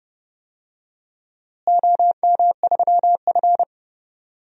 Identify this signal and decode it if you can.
Morse code, heard as a keyed tone, about 30 wpm, 700 Hz. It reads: OM3F
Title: DXCC-100 Call Signs - 4 Characters - Repeated 2x - 30wpm